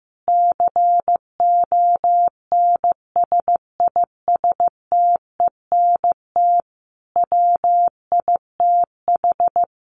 Morse code, heard as a keyed tone, about 15 wpm, 700 Hz. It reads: CONSISTENT WITH
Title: Sentences from Top 1500 Words 15wpm